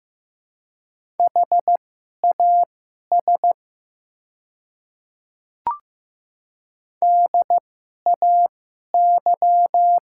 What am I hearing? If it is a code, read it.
Morse code, heard as a keyed tone, about 15 wpm, spaced as for 11 wpm, 700 Hz. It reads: HAS DAY
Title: 2x Character Spacing - Top 100 Words - 15wpm